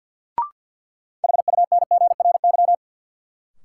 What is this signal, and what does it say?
Morse code, heard as a keyed tone, about 53 wpm, 700 Hz. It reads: HUNGRY
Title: Sentences from Top 1400 Words - New Word Drill - 50wpm